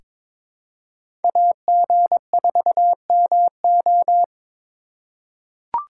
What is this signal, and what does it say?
Morse code, heard as a keyed tone, about 22 wpm, 700 Hz. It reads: AG4MO
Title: USA Call Signs - 5 Characters - 8x Character Spacing - 22wpm